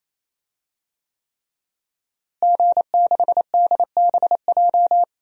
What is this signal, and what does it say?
Morse code, heard as a keyed tone, about 27 wpm, 700 Hz. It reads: G6DBJ
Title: DXCC-100 Call Signs - 5 Characters - Repeated 3x - 28wpm